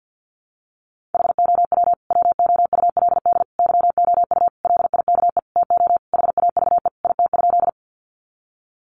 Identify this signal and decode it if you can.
Morse code, heard as a keyed tone, about 50 wpm, 700 Hz. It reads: HOW WOULD YOU LIKE TO HAVE IT?
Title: Sentences from Top 100 Words 50wpm